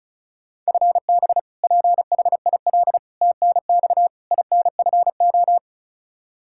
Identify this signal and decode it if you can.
Morse code, heard as a keyed tone, about 36 wpm, 700 Hz. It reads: FB PHIL TNX INFO
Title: Practice QSO - Exchange 3 - More Info - 35wpm